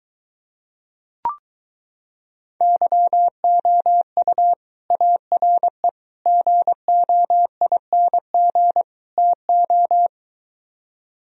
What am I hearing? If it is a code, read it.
Morse code, heard as a keyed tone, about 23 wpm, 700 Hz. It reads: YOU ARE GOING TO